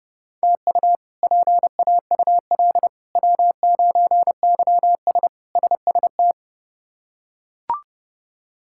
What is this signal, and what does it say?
Morse code, heard as a keyed tone, about 32 wpm, 700 Hz. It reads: TU PAUL W9YS SST
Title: K1USN SST QSO Practice - 30wpm